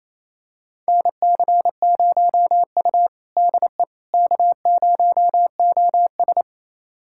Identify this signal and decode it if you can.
Morse code, heard as a keyed tone, about 28 wpm, 700 Hz. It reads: NC0U DE K0OS
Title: USA Callsign DE Callsign - 4 Characters - 28wpm